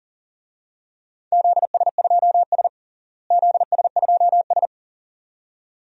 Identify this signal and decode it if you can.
Morse code, heard as a keyed tone, about 40 wpm, 700 Hz. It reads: ZS2S ZS2S
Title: DXCC-100 Call Signs - 4 Characters - Repeated 2x - 40wpm